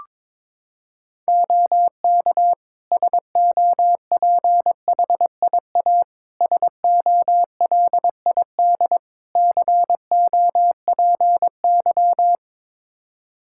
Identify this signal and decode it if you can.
Morse code, heard as a keyed tone, about 22 wpm, 700 Hz. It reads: OK SOPHIA SOLID COPY
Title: Practice QSO - Exchange 3 - More Info - 22wpm